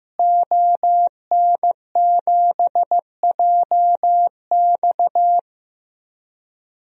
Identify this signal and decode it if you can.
Morse code, heard as a keyed tone, about 15 wpm, 700 Hz. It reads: ON7JX
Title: DXCC-100 Call Signs - 5 Characters - Repeated 3x - 15wpm